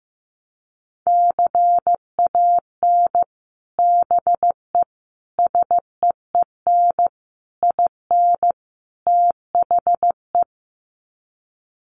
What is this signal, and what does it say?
Morse code, heard as a keyed tone, about 15 wpm, 700 Hz. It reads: CAN BE SEEN IN THE
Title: Sets of 5 Words Encore2 15wpm